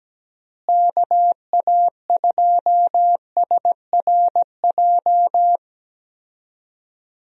Morse code, heard as a keyed tone, about 17 wpm, 700 Hz. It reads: KA2SRJ
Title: USA Call Signs - 6 Characters - Repeated 3x - 17wpm